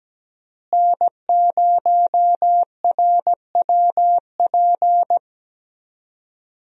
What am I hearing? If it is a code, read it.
Morse code, heard as a keyed tone, about 17 wpm, 700 Hz. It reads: N0RWP
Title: USA Call Signs - 5 Characters - Repeated 2x - 17wpm